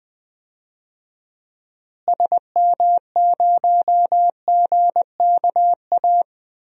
Morse code, heard as a keyed tone, about 20 wpm, 700 Hz. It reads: SM0GKA
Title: DXCC-100 Call Signs - 6 Characters - Repeated 3x - 20wpm